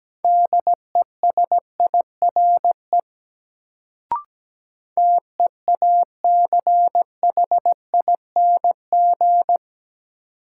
Morse code, heard as a keyed tone, about 17 wpm, 700 Hz. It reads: DESIRE TEACHING
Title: Sentences from Top 1100 Words - New Word Drill Alt - 17wpm